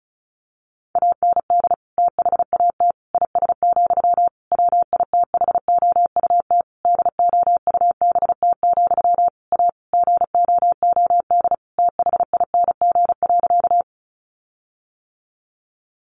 Morse code, heard as a keyed tone, about 34 wpm, 700 Hz. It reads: AND THAT IS, WITHOUT DOUBT, A GOOD THING.
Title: Sentences from Top 1500 Words 35wpm